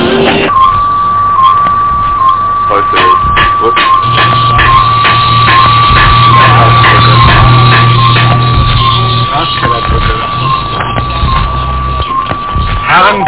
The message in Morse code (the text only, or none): none